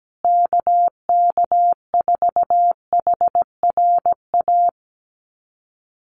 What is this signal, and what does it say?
Morse code, heard as a keyed tone, about 17 wpm, 700 Hz. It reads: KK4HRA
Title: USA Call Signs Repeated 2x 17wpm